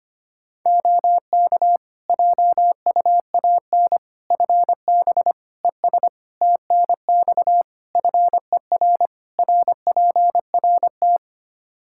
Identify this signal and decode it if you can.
Morse code, heard as a keyed tone, about 25 wpm, 700 Hz. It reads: OK JUAN FB ES TNX FER RPRT
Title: K1USN SST QSO Practice 25wpm